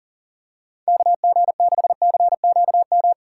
Morse code, heard as a keyed tone, about 40 wpm, 700 Hz. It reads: KG6CQM
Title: USA Call Signs - 3x Character Spacing - 40wpm